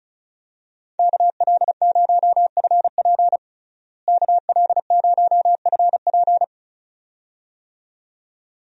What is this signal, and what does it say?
Morse code, heard as a keyed tone, about 36 wpm, 700 Hz. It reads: KL0FP KL0FP
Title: USA Call Signs - 5 Characters - Repeated 2x - 35wpm